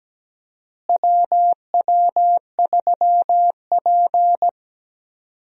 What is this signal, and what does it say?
Morse code, heard as a keyed tone, about 17 wpm, 700 Hz. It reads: WW3P
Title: USA Call Signs - 4 Characters - Repeated 3x - 17wpm